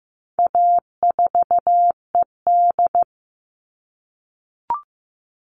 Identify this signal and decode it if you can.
Morse code, heard as a keyed tone, about 15 wpm, 700 Hz. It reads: A4ED
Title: All Cut Numbers - 4-Digit 15wpm